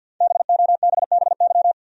Morse code, heard as a keyed tone, about 52 wpm, 700 Hz. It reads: BOBBY
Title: USA Common Names - Max 5 Letters - 50wpm